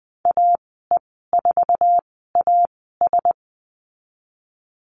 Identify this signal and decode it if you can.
Morse code, heard as a keyed tone, about 20 wpm, spaced as for 15 wpm, 700 Hz. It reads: AE4AS
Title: 055 Lesson 10g - Letter r - Call Signs 20wpm